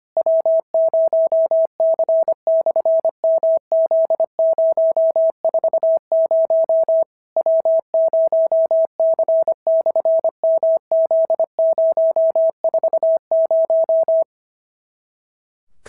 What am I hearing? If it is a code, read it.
Morse code, heard as a keyed tone, about 25 wpm, 650 Hz. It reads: W0C/MZ040 W0C/MZ040
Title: SOTA QSO Practice 25wpm